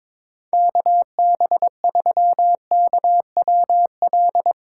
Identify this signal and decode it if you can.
Morse code, heard as a keyed tone, about 22 wpm, 700 Hz. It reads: KB3KWL
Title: USA Call Signs - 6 Characters - 8x Character Spacing - 22wpm